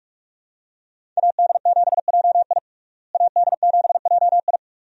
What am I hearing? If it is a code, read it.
Morse code, heard as a keyed tone, about 46 wpm, 700 Hz. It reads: AD7JI AD7JI
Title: USA Call Signs - 5 Characters - Repeated 2x - 45wpm